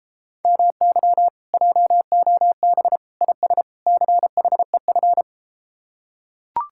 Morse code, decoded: MY JOB IS CHEF

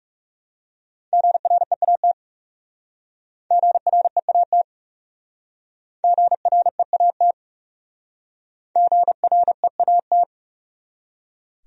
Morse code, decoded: GREAT GREAT GREAT GREAT

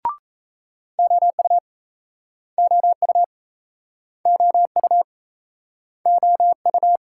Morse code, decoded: OU OU OU OU

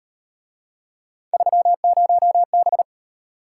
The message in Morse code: 30B

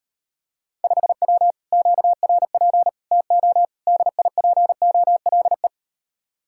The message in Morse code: 5W QRP TO DIPOLE

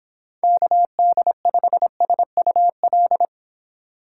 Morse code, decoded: KD5SUL

